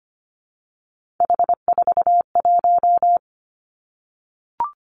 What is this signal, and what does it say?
Morse code, heard as a keyed tone, about 25 wpm, 700 Hz. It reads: H41
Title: Three Challenging Letter-Number Combinations 25wpm